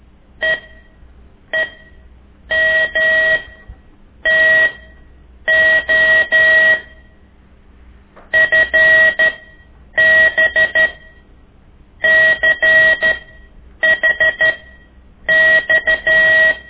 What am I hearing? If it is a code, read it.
Morse code, heard as a keyed tone, about 12 wpm, spaced as for 7 wpm, 1900 Hz. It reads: EEMTO FBCHX
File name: ik1weg2 / CORSI CW